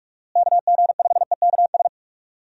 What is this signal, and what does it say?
Morse code, heard as a keyed tone, about 41 wpm, 700 Hz. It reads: KG5EKS
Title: USA Call Signs - 6 Characters - Repeated 2x - 45wpm